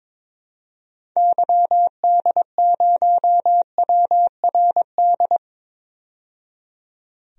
Morse code, heard as a keyed tone, about 22 wpm, 700 Hz. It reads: YD0WRD
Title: DXCC-100 Call Signs - 6 Characters - 22wpm